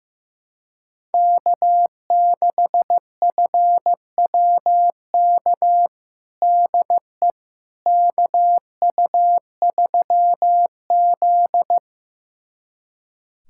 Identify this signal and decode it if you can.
Morse code, heard as a keyed tone, about 15 wpm, 700 Hz. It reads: K6FWK DE KU3Z
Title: USA Callsign DE Callsign 15wpm